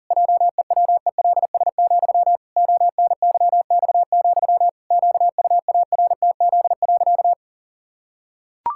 Morse code, heard as a keyed tone, about 40 wpm, 700 Hz. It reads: JEWELS, ONYX, QUARTZ.
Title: Pangram Challenge v2 - 40wpm